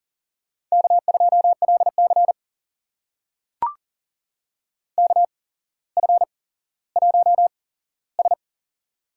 Morse code, heard as a keyed tone, about 40 wpm, spaced as for 17 wpm, 700 Hz. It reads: K2LC KF1S
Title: USA Call Signs - 4 Characters - 8x Character Spacing - 40wpm